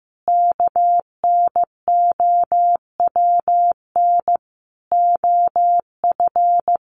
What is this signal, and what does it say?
Morse code, heard as a keyed tone, about 15 wpm, 700 Hz. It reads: KNOWN OF
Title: Sentences from Top 1500 Words 15wpm